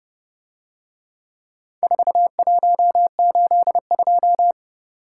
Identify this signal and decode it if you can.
Morse code, heard as a keyed tone, about 29 wpm, 700 Hz. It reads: 4182